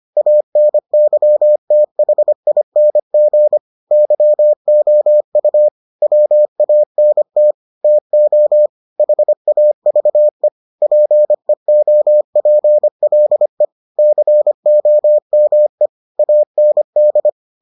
Morse code, heard as a keyed tone, about 25 wpm, 600 Hz. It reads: ANYTHING YOU WANT TO HAVE PEOPLE COME AND